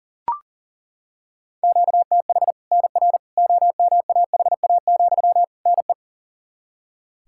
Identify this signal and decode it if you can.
Morse code, heard as a keyed tone, about 40 wpm, 700 Hz. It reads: QTH NR OMAHA, NE
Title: Practice QSO - Exchange 1 - Beginning - 40wpm